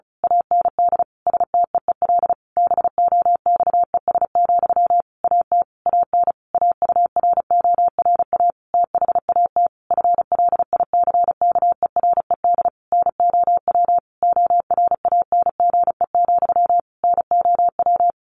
Morse code, heard as a keyed tone, about 34 wpm, 700 Hz. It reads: AND STEEL BOXES, AT AN AURORA THAT FLICKERED NOW ORANGE, NOW
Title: A Sound Of Thunder 35wpm